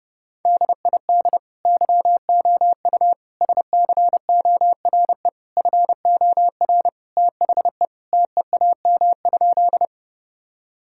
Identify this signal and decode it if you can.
Morse code, heard as a keyed tone, about 30 wpm, 700 Hz. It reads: DID YOU SCORE FOR THE TEAM?
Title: Sentences from Top 1600 Words 30wpm